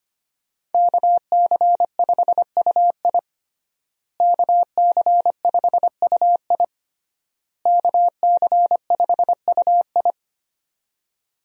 Morse code, heard as a keyed tone, about 25 wpm, 700 Hz. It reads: KC5UI KC5UI KC5UI
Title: USA Call Signs - 5 Characters - Repeated 3x - 25wpm